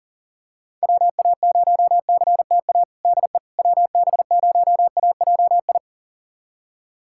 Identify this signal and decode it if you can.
Morse code, heard as a keyed tone, about 37 wpm, 700 Hz. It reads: WA0CTA DE WB0AJI
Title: USA Callsign DE Callsign - 6 Characters - 40wpm